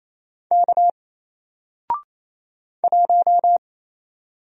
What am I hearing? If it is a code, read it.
Morse code, heard as a keyed tone, about 27 wpm, 700 Hz. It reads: K 1
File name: Single-Letter-Number-28wpm